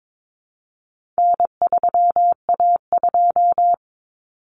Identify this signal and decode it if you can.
Morse code, heard as a keyed tone, about 22 wpm, 700 Hz. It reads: N3A2
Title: Common Cut Numbers - 4-Digit 22wpm